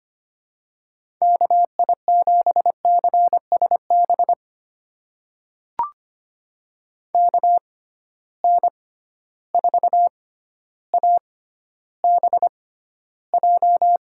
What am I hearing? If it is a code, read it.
Morse code, heard as a keyed tone, about 25 wpm, spaced as for 13 wpm, 700 Hz. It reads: KI7CSB KN4ABJ
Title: USA Call Signs - 6 Characters - 6x Character Spacing - 25wpm